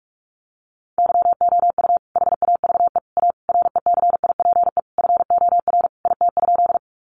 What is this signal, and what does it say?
Morse code, heard as a keyed tone, about 44 wpm, 700 Hz. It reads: YOU HAVE A RECIPE FOR IT?
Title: Sentences from Top 1200 Words 45wpm